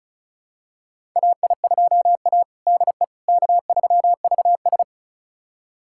Morse code, heard as a keyed tone, about 34 wpm, 700 Hz. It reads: AI2A DE K3VS
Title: USA Callsign DE Callsign - 4 Characters - 35wpm